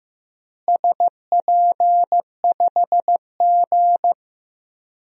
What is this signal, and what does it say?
Morse code, heard as a keyed tone, about 15 wpm, 700 Hz. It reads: SP5G
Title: DXCC-100 Call Signs - 4 Characters - Repeated 3x - 15wpm